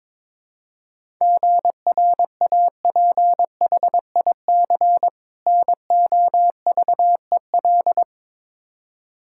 Morse code, heard as a keyed tone, about 22 wpm, 700 Hz. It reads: GRAPHIC NOVEL
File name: Difficult Words with Hints - 22wpm